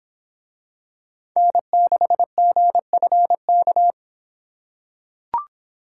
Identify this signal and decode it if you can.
Morse code, heard as a keyed tone, about 26 wpm, 700 Hz. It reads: N6GFK